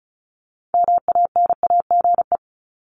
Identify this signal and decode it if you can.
Morse code, heard as a keyed tone, about 34 wpm, 700 Hz. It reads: MANAGE
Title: Top 1251-1500 Words 35wpm